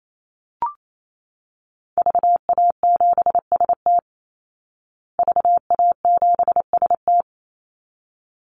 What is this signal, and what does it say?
Morse code, heard as a keyed tone, about 28 wpm, 700 Hz. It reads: VA7ST VA7ST
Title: Call Signs Repeated 3x - 28wpm